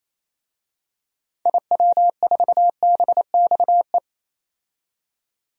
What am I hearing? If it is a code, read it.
Morse code, heard as a keyed tone, about 27 wpm, 700 Hz. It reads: IW4BXE
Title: DXCC-100 Call Signs - 6 Characters - 28wpm